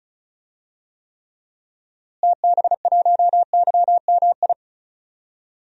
Morse code, heard as a keyed tone, about 36 wpm, 700 Hz. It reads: TB1YMI
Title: DXCC-100 Call Signs - 6 Characters - Repeated 2x - 35wpm